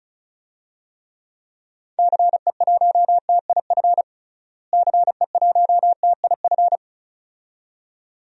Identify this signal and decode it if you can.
Morse code, heard as a keyed tone, about 36 wpm, 700 Hz. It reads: CE1TIF CE1TIF
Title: DXCC-100 Call Signs - 6 Characters - Repeated 2x - 35wpm